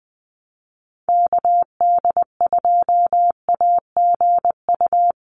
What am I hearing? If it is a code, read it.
Morse code, heard as a keyed tone, about 20 wpm, 700 Hz. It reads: KD2AGU
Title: USA Call Signs 20wpm